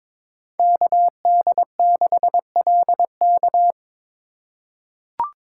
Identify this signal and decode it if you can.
Morse code, heard as a keyed tone, about 21 wpm, 700 Hz. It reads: KD6LK
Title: USA Call Signs - 5 Characters - 2x Character Spacing - 22wpm